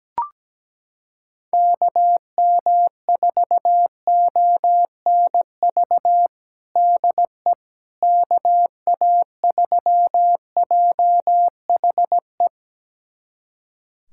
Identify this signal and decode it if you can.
Morse code, heard as a keyed tone, about 17 wpm, 700 Hz. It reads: KM4ONV DE KA3JHE
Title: USA Callsign DE Callsign - 6 Characters - 17wpm